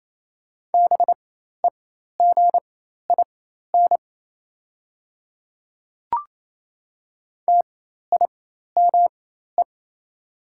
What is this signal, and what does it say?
Morse code, heard as a keyed tone, about 28 wpm, spaced as for 13 wpm, 690 Hz. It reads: BEGIN TIME
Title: 4x Character Spacing - Top 500 Words - 28wpm